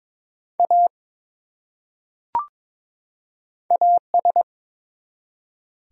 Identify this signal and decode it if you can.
Morse code, heard as a keyed tone, about 21 wpm, 700 Hz. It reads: A AS
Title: Top 100 Words 22wpm